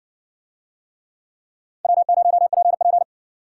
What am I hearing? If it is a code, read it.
Morse code, heard as a keyed tone, about 65 wpm, 700 Hz. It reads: W0PP